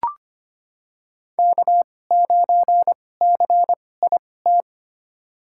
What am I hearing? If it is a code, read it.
Morse code, heard as a keyed tone, about 25 wpm, spaced as for 18 wpm, 700 Hz. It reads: K9CIT
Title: USA Call Signs - 5 Characters - 2x Character Spacing - 25wpm